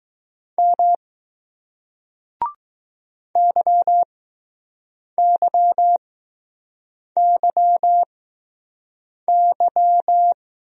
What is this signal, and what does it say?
Morse code, heard as a keyed tone, about 20 wpm, spaced as for 6 wpm, 700 Hz. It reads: M YYYY